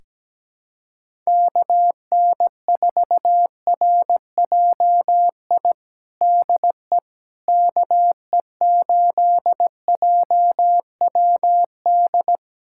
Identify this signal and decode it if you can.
Morse code, heard as a keyed tone, about 17 wpm, 700 Hz. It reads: KN4RJI DE KE8JWD
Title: USA Callsign DE Callsign - 6 Characters - 17wpm